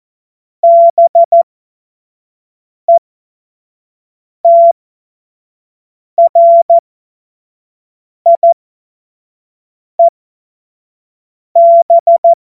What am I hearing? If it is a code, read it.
Morse code, heard as a keyed tone, about 14 wpm, spaced as for 5 wpm, 680 Hz. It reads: BETRIEB
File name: Morsekurs / Witze / Witz 15. November 2022 25 F70